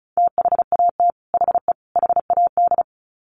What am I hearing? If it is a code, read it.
Morse code, heard as a keyed tone, about 34 wpm, 700 Hz. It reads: THAT HE HAD
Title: Sets of 3 Words 35wpm